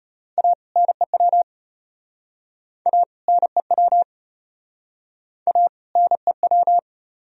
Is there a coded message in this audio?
Yes: Morse code from 0.4 to 6.8 s, heard as a keyed tone, about 35 wpm, 700 Hz. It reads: A NEW A NEW A NEW